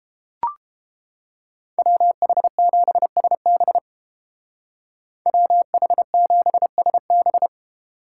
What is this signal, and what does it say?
Morse code, heard as a keyed tone, about 31 wpm, 700 Hz. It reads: WH7SB WH7SB